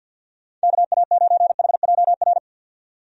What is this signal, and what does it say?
Morse code, heard as a keyed tone, about 50 wpm, 700 Hz. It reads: KA9HJR
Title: USA Call Signs - 6 Characters - 50wpm